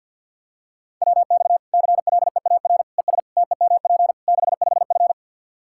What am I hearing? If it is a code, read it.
Morse code, heard as a keyed tone, about 48 wpm, 700 Hz. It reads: WX CLEAR ES TEMP 65F
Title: Practice QSO - Exchange 2 - Info - 50wpm